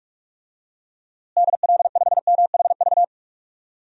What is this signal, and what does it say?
Morse code, heard as a keyed tone, about 46 wpm, 700 Hz. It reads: DL5MHV
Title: DXCC-100 Call Signs - 6 Characters - Repeated 2x - 45wpm